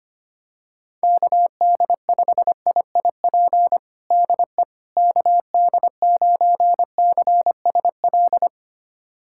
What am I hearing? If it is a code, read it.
Morse code, heard as a keyed tone, about 25 wpm, 700 Hz. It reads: KD5IIP DE KD9CSL
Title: USA Callsign DE Callsign - 6 Characters - 25wpm